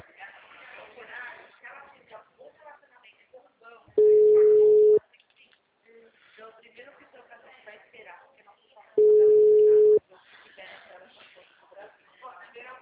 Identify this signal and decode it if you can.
Ringback tone (Europe, 425 Hz), with speech.
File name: out- -2010- - - .32